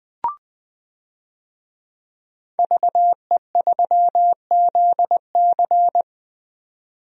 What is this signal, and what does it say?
Morse code, heard as a keyed tone, about 20 wpm, 700 Hz. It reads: VE3ZC
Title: DXCC-100 Call Signs - 5 Characters - 20wpm